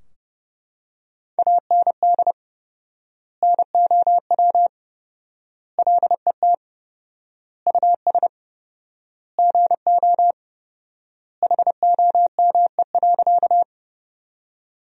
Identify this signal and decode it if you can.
Morse code, heard as a keyed tone, about 30 wpm, 700 Hz. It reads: AND NOW LET US GO HOME.
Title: 4x Word-Spacing - Sentences from Top 300 Words - 30wpm